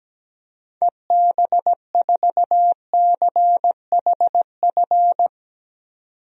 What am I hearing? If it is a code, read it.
Morse code, heard as a keyed tone, about 17 wpm, 700 Hz. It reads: EB4CHF
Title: DXCC-100 Call Signs - 6 Characters - 17wpm